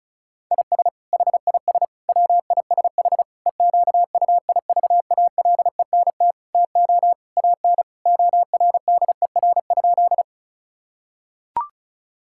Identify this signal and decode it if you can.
Morse code, heard as a keyed tone, about 36 wpm, 700 Hz. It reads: IS HIS WISH EQUIVALENT TO AN ORDER?
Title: Sentences from Top 1600 Words 35wpm